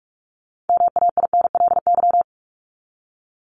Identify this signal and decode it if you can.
Morse code, heard as a keyed tone, about 44 wpm, 700 Hz. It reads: MAINLY
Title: LY Suffix - Top 100 Words with Base Words 45wpm